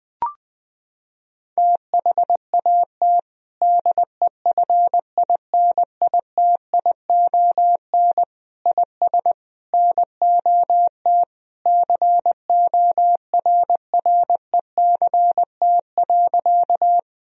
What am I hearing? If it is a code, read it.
Morse code, heard as a keyed tone, about 20 wpm, 700 Hz. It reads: THAT DEFINITION IS NOT CORRECT.
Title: Sentences from Top 1000 Words - Review Part 4 - 20wpm